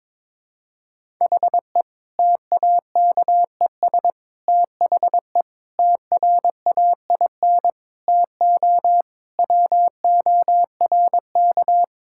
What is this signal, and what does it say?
Morse code, heard as a keyed tone, about 22 wpm, 700 Hz. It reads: HE TAKES THE TRAIN TO WORK
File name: Difficult Sentences with Hints - 22wpm